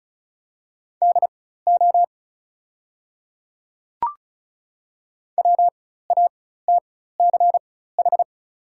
Morse code, heard as a keyed tone, about 36 wpm, spaced as for 16 wpm, 690 Hz. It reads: DO WATCH
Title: 4x Character Spacing - Top 300 Words - 35wpm